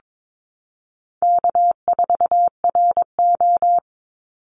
Morse code, heard as a keyed tone, about 22 wpm, 700 Hz. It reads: K4RO
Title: Call Signs Repeated 3x - 22wpm